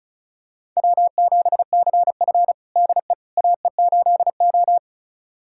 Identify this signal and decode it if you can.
Morse code, heard as a keyed tone, about 36 wpm, 700 Hz. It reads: W7CF DE AE8O
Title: USA Callsign DE Callsign - 4 Characters - 35wpm